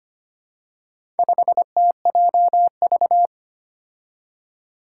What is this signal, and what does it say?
Morse code, heard as a keyed tone, about 25 wpm, 700 Hz. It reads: 5TJV
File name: practice / 25wpm / Four Letter-Number Combinations - 25wpm